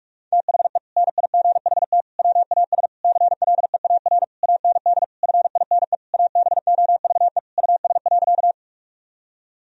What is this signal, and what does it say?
Morse code, heard as a keyed tone, about 41 wpm, 700 Hz. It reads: THE NIGHT WAS CLEAR AND FINE ABOVE US.
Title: Sentences from Top 1000 Words - Review Part 2 - 45wpm